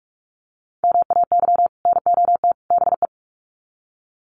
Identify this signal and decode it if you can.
Morse code, heard as a keyed tone, about 44 wpm, 700 Hz. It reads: MAY NOT BE
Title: Sets of 3 Words 45wpm